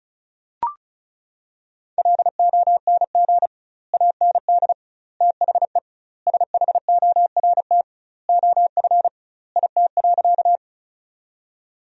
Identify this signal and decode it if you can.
Morse code, heard as a keyed tone, about 34 wpm, 700 Hz. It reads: LONG AND THE SHORT OF IT.
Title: 2x Word-Spacing - Sentences from Top 400 Words - 35wpm